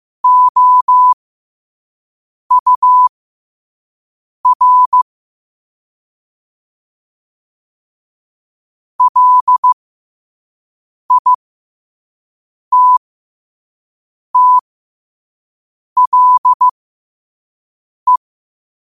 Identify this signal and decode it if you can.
Morse code, heard as a keyed tone, about 15 wpm, spaced as for 5 wpm, 1000 Hz. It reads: OUR LITTLE
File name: qotdfiles / 5wpm